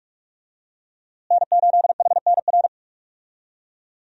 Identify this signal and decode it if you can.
Morse code, heard as a keyed tone, about 44 wpm, 700 Hz. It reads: N8HNR
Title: USA Call Signs - 5 Characters - Repeated 3x - 45wpm